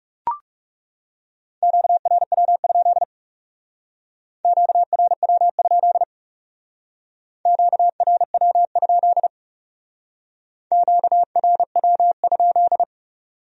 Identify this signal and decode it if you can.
Morse code, heard as a keyed tone, about 39 wpm, 700 Hz. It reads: QRW? QRW? QRW? QRW?